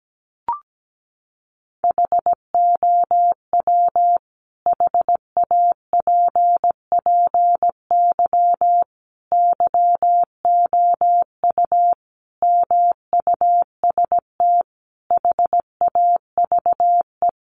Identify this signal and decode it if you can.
Morse code, heard as a keyed tone, about 17 wpm, 700 Hz. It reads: HOW HAPPY YOU MUST HAVE
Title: Sentences from Top 1000 Words 17wpm